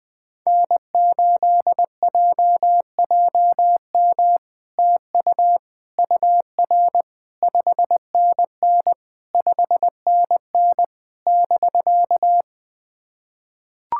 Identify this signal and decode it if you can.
Morse code, heard as a keyed tone, about 19 wpm, 700 Hz. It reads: N8JJM TU UR 5NN 5NN <BK>
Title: POTA QSO Practice - 20wpm